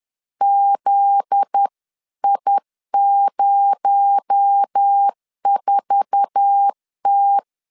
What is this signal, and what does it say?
Morse code, heard as a keyed tone, about 10 wpm, 800 Hz. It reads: ZI04T
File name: LEZIONI / 8 YJ45 / 8 YJ4 10wpm 2633